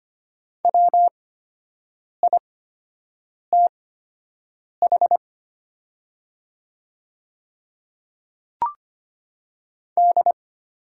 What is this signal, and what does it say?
Morse code, heard as a keyed tone, about 25 wpm, spaced as for 7 wpm, 700 Hz. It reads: WITH D
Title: 8x Character Spacing - Top 100 Words - 25wpm